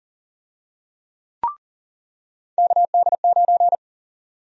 Morse code, heard as a keyed tone, about 40 wpm, 700 Hz. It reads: KD9
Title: USA Call Sign Prefixes - 40wpm